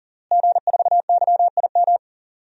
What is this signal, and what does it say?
Morse code, heard as a keyed tone, about 40 wpm, 700 Hz. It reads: G4YIM
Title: DXCC-100 Call Signs - 5 Characters - Repeated 3x - 40wpm